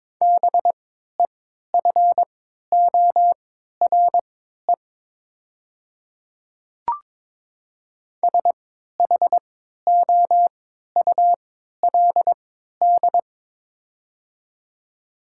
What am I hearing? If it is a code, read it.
Morse code, heard as a keyed tone, about 22 wpm, spaced as for 12 wpm, 700 Hz. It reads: BEFORE SHOULD
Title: 3x Character Spacing - Top 300 Words - 22wpm